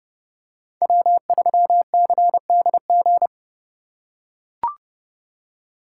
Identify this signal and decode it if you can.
Morse code, heard as a keyed tone, about 30 wpm, 700 Hz. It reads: W3CDG
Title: USA Call Signs - 5 Characters - 4x Character Spacing - 30wpm